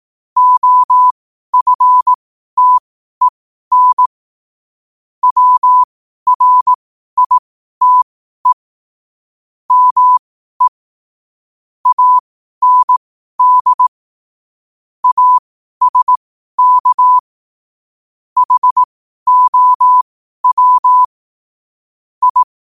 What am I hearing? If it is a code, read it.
Morse code, heard as a keyed tone, about 18 wpm, spaced as for 12 wpm, 1000 Hz. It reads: OFTEN WRITE ME AND ASK HOW I
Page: New quotes every day in morse code at 12 Words per minute.